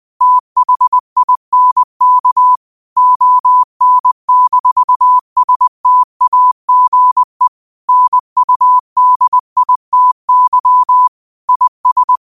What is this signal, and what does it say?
Morse code, heard as a keyed tone, about 20 wpm, 1000 Hz. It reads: THINK ON-STAGE NUDITY IS
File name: qotdfiles / 20wpm